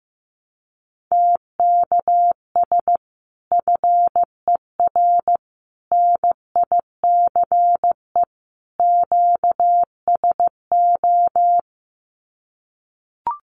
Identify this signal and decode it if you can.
Morse code, heard as a keyed tone, about 15 wpm, 700 Hz. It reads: TKS FER NICE QSO
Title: Common QSO Phrases 15wpm